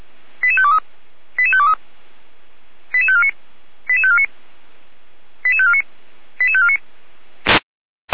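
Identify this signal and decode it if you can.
Five-tone selective call (ZVEI), heard as quick runs of five tones, tones of about 70 ms. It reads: 88523 88523 88549 88549 88549 88549
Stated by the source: ZVEI five-tone selective call received by SDR.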